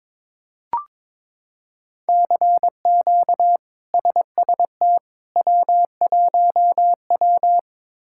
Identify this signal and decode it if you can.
Morse code, heard as a keyed tone, about 22 wpm, 700 Hz. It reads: CQ SST W1W
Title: K1USN SST QSO Practice - 22wpm